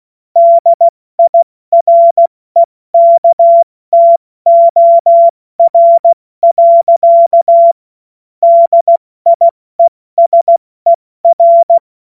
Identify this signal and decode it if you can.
Morse code, heard as a keyed tone, about 16 wpm, 680 Hz. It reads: DIREKTOR. DIESER
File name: Morsekurs / Witze / Witz 13. September 2022 70 F80